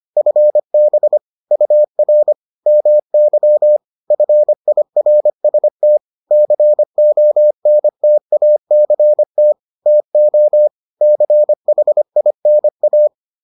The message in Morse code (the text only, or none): FB UR MY FIRST CONTACT TO CHINA